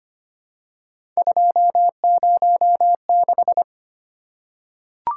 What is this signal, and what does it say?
Morse code, heard as a keyed tone, about 25 wpm, 700 Hz. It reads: 206
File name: Three Letter-Number Combinations - 25wpm